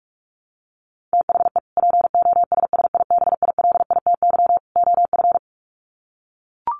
Morse code, heard as a keyed tone, about 44 wpm, 700 Hz. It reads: THE POSSIBILITY OF
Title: Sets of 3 Words 45wpm